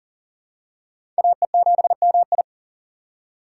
Morse code, heard as a keyed tone, about 40 wpm, 700 Hz. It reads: AE7MI
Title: USA Call Signs - 5 Characters - 4x Character Spacing - 40wpm